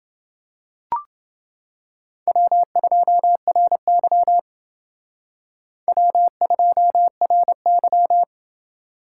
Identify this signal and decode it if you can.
Morse code, heard as a keyed tone, about 28 wpm, 700 Hz. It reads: W2RY W2RY